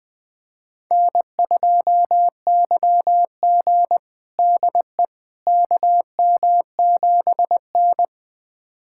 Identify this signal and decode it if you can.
Morse code, heard as a keyed tone, about 20 wpm, 700 Hz. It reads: N2YG DE KM7N
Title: USA Callsign DE Callsign - 4 Characters - 20wpm